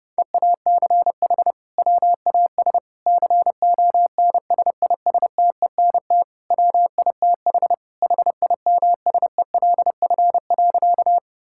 Morse code, heard as a keyed tone, about 30 wpm, 700 Hz. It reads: EACH WAS CONSISTENT WITH HIMSELF.
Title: Sentences from Top 1600 Words 30wpm